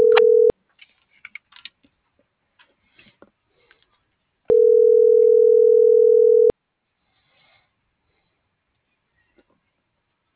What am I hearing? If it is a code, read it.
Ringback tone (North America), 440 + 480 Hz.